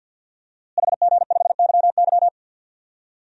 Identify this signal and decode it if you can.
Morse code, heard as a keyed tone, about 51 wpm, 700 Hz. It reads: HG5YY
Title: DXCC-100 Call Signs - 5 Characters - Repeated 3x - 50wpm